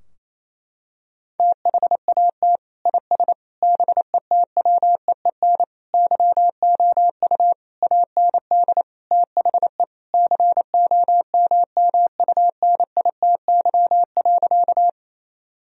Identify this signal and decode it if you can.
Morse code, heard as a keyed tone, about 28 wpm, 700 Hz. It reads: THAT IS BETWEEN YOU AND THE COMMUNITY.
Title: Sentences from Top 1000 Words - Review Part 4 - 28wpm